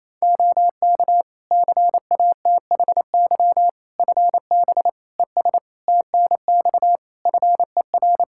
Morse code, heard as a keyed tone, about 28 wpm, 700 Hz. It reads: OK CATHY FB ES TNX FER
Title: K1USN SST QSO Practice 28wpm